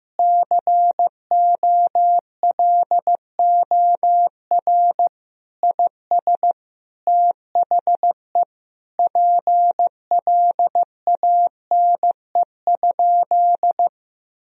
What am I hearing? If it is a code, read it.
Morse code, heard as a keyed tone, about 16 wpm, 700 Hz. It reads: COLOR IS THE PLANE?
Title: Sentences from Top 1000 Words - Review Part 3 - 15wpm